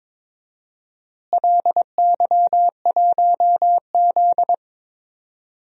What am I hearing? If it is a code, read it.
Morse code, heard as a keyed tone, about 22 wpm, 700 Hz. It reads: LY1Z
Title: DXCC-100 Call Signs - 4 Characters - Repeated 2x - 22wpm